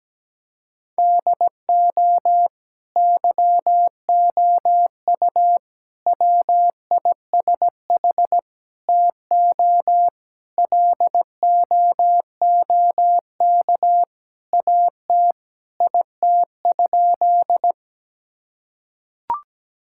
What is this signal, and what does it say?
Morse code, heard as a keyed tone, about 17 wpm, 700 Hz. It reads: DO YOU WISH TO LOOK AT IT?
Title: Sentences from Top 1600 Words 17wpm